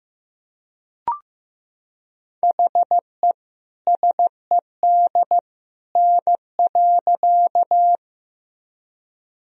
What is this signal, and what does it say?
Morse code, heard as a keyed tone, about 16 wpm, 700 Hz. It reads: HE SED N.
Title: Sentences from Top 100 Words and Abbreviations 15wpm